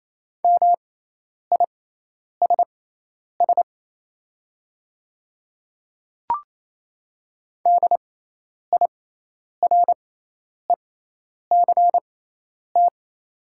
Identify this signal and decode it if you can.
Morse code, heard as a keyed tone, about 28 wpm, spaced as for 10 wpm, 700 Hz. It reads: MISS DIRECT
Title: 6x Character Spacing - Top 500 Words - 28wpm